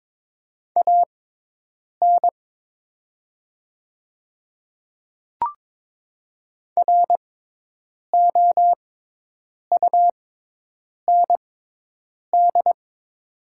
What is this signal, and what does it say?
Morse code, heard as a keyed tone, about 22 wpm, spaced as for 8 wpm, 700 Hz. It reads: AN ROUND
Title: 6x Character Spacing - Top 300 Words - 22wpm